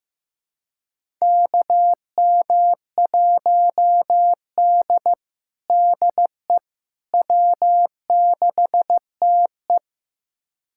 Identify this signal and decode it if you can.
Morse code, heard as a keyed tone, about 15 wpm, 700 Hz. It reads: KM1D DE W6TE
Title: USA Callsign DE Callsign - 4 Characters - 15wpm